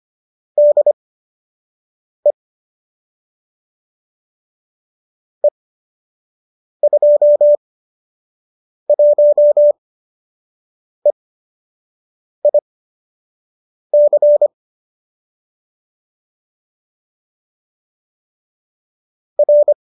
Morse code, heard as a keyed tone, about 25 wpm, spaced as for 6 wpm, 570 Hz. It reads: DE E21EIC R